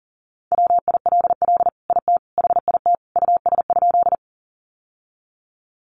Morse code, heard as a keyed tone, about 40 wpm, 700 Hz. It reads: WILL IT HIT US?
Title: Sentences from Top 1200 Words 40wpm